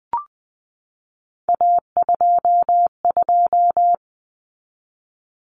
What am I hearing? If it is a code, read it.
Morse code, heard as a keyed tone, about 20 wpm, 700 Hz. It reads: A22
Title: Common Cut Numbers - 3-Digit 20wpm